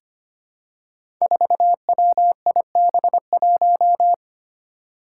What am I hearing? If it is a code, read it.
Morse code, heard as a keyed tone, about 25 wpm, 700 Hz. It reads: 4WIB1
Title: Five Challenging Letter-Number Combinations 25wpm